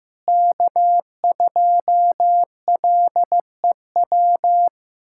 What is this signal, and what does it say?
Morse code, heard as a keyed tone, about 15 wpm, 700 Hz. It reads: K2LEW
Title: USA Callsign DE Callsign - 5 Characters - 15wpm